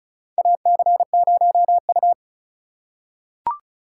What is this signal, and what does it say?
Morse code, heard as a keyed tone, about 34 wpm, 700 Hz. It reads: AC0U
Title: USA Call Signs - 4 Characters - 2x Character Spacing - 35wpm